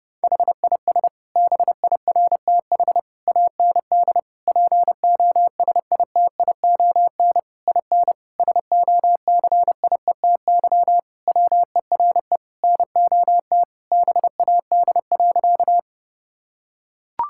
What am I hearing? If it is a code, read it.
Morse code, heard as a keyed tone, about 32 wpm, 700 Hz. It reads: HIS BIRTH AND POSITION IN SOCIETY WERE NOT BAD.
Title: Sentences from Top 1000 Words - Review Part 5 - 30wpm